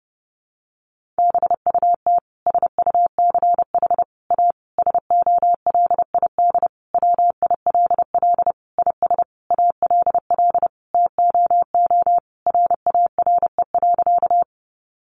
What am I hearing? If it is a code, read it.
Morse code, heard as a keyed tone, about 28 wpm, 700 Hz. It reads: BUT SUCH A SOLID WILL IS ALL TOO RARE.
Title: Sentences from Top 1500 Words 30wpm